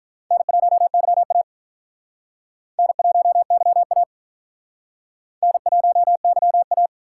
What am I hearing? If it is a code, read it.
Morse code, heard as a keyed tone, about 48 wpm, 700 Hz. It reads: N1YA N1YA N1YA